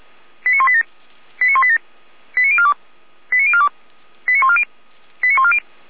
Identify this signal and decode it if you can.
Five-tone selective call (ZVEI), heard as quick runs of five tones, tones of about 80 ms. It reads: 89187 89187 89042 89042 89140 89140
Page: ZVEI